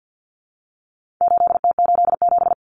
Morse code, heard as a keyed tone, about 48 wpm, 700 Hz. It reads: 8T87
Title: Common Cut Numbers - 4-Digit 50wpm